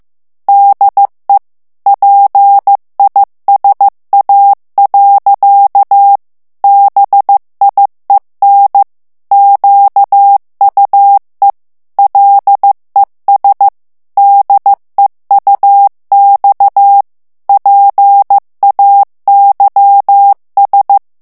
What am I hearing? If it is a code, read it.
Morse code, heard as a keyed tone, about 15 wpm, 800 Hz. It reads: DE PISA. BIEN QUE LES DEUX PAYS
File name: morse mp3 / FRANCAIS TEXTE CLAIR 1 VITESSE 14 MOTS MINUTES